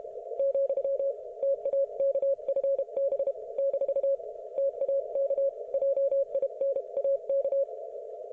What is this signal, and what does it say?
Morse code, heard as a keyed tone, about 32 wpm, 550 Hz. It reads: , TAK FB - TAK JINAK